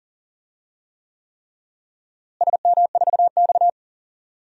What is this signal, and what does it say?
Morse code, heard as a keyed tone, about 40 wpm, 700 Hz. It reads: SM4X
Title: DXCC-100 Call Signs - 4 Characters - Repeated 3x - 40wpm